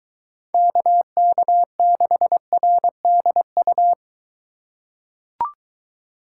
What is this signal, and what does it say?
Morse code, heard as a keyed tone, about 21 wpm, 700 Hz. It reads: KK6RDU